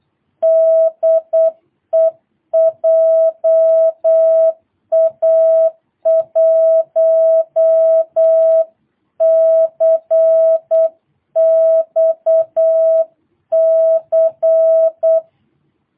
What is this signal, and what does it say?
Morse code, heard as a keyed tone, about 8 wpm, 650 Hz. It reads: DE JA1CXC
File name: jh8cbh / 2022cw